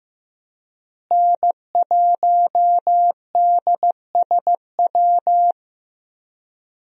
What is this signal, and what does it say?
Morse code, heard as a keyed tone, about 15 wpm, 700 Hz. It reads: N1DSW
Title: USA Call Signs - 5 Characters - Repeated 3x - 15wpm